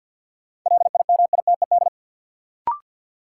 Morse code, heard as a keyed tone, about 47 wpm, 700 Hz. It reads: LIMITED
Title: Sentences from Top 900 Words - New Word Drill - 50wpm